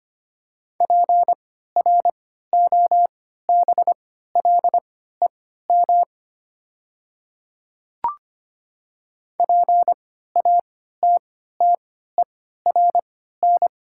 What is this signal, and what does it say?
Morse code, heard as a keyed tone, about 25 wpm, spaced as for 14 wpm, 700 Hz. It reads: PROBLEM PATTERN
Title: 3x Character Spacing - Top 500 Words - 25wpm